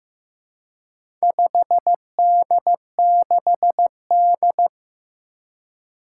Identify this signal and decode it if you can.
Morse code, heard as a keyed tone, about 15 wpm, 700 Hz. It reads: 5D6D
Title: Four Challenging Letter-Number Combinations 15wpm